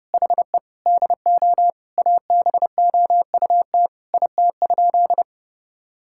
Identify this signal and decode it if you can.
Morse code, heard as a keyed tone, about 30 wpm, 700 Hz. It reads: HE DO ABOUT IT?
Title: Sentences from Top 1000 Words - Review Part 6 - 30wpm